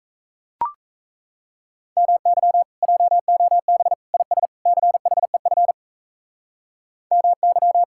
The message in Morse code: MY JOB IS CHEF MY

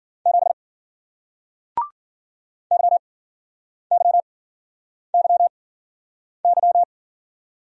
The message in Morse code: 7 YYYY

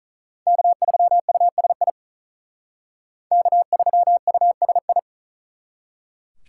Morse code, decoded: K3USI K3USI

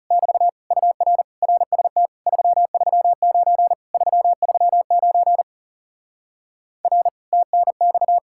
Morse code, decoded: <BT> UR RST 339 339 R TNX

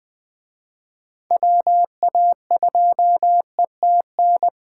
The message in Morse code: WA2ETN

